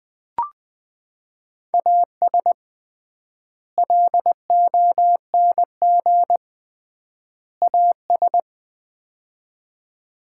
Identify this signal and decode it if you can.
Morse code, heard as a keyed tone, about 20 wpm, 700 Hz. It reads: AS LONG AS
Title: 3x Word-Spacing - Sets of 3 Words - 20wpm